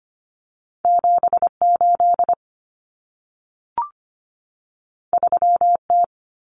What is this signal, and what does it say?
Morse code, heard as a keyed tone, about 25 wpm, 700 Hz. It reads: 78 3T
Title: Common Cut Numbers - 2-Digit 25wpm